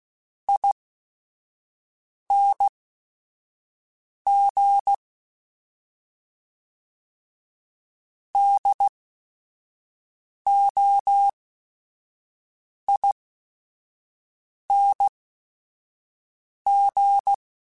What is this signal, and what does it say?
Morse code, heard as a keyed tone, about 16 wpm, spaced as for 5 wpm, 800 Hz. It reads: ING DOING
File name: Special INGs / Special INGs-mp3s / Sngl Ing @5